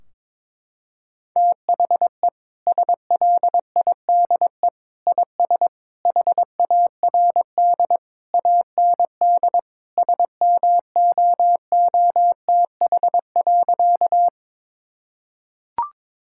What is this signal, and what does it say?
Morse code, heard as a keyed tone, about 22 wpm, 700 Hz. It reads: THE SLIDE IS HARD AND SMOOTH.
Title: Sentences from Top 1600 Words 22wpm